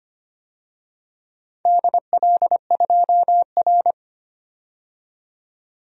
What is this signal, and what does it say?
Morse code, heard as a keyed tone, about 25 wpm, 700 Hz. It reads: DL2R
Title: DXCC-100 Call Signs - 4 Characters - 25wpm